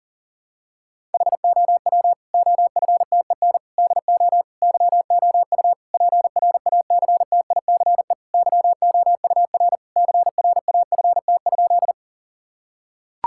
Morse code, heard as a keyed tone, about 37 wpm, 700 Hz. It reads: HOW OFTEN DO YOU PRACTICE YOUR CRAFT?
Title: Sentences from Top 1000 Words - Review Part 6 - 40wpm